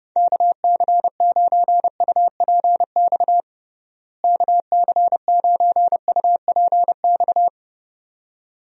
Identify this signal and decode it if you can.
Morse code, heard as a keyed tone, about 29 wpm, 700 Hz. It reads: KC9UPX KC9UPX
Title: USA Call Signs - 6 Characters - Repeated 2x - 30wpm